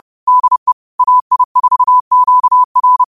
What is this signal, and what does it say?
Morse code, heard as a keyed tone, about 30 wpm, 1000 Hz. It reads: DE AI4QR
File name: qotdfiles / 30wpm